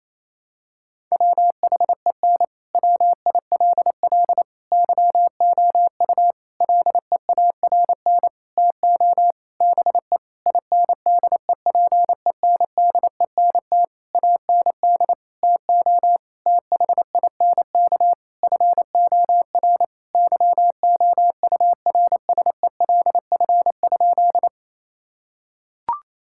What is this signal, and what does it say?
Morse code, heard as a keyed tone, about 27 wpm, 700 Hz. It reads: WHEN WILL YOU LEARN TO BE INDEPENDENT AND TO THINK FOR YOURSELF?
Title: Sentences from Top 1000 Words - Review Part 6 - 28wpm